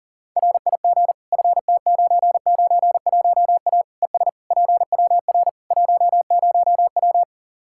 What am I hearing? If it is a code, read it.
Morse code, heard as a keyed tone, about 40 wpm, 700 Hz. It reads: RIG FT991A ES PWR 10W
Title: Practice QSO - Exchange 2 - Info - 40wpm